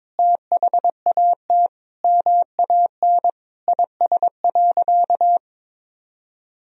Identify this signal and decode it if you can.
Morse code, heard as a keyed tone, about 22 wpm, 700 Hz. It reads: THAT MAN IS.
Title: Sentences from Top 1000 Words - Review Part 2 - 22wpm